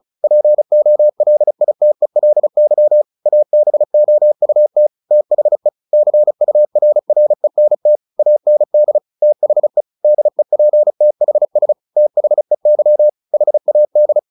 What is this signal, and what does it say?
Morse code, heard as a keyed tone, about 35 wpm, 600 Hz. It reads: POLITELY ABOUT THE CURRENT AND THE DEPTHS THEY HAD
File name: the old man and the sea / the old man and the sea 1 35wpm